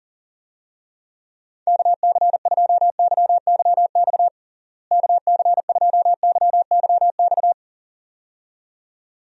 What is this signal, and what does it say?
Morse code, heard as a keyed tone, about 37 wpm, 700 Hz. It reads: KC2YYX KC2YYX
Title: USA Call Signs - 6 Characters - Repeated 2x - 40wpm